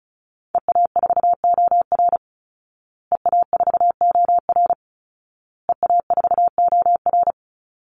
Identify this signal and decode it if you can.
Morse code, heard as a keyed tone, about 34 wpm, 700 Hz. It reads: EA4OR EA4OR EA4OR
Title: Call Signs Repeated 2x - 35wpm